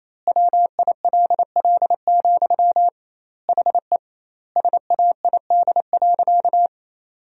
Morse code, heard as a keyed tone, about 28 wpm, 700 Hz. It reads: WILL, HE SAID.
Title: 2x Word-Spacing - Sentences from Top 200 Words - 28wpm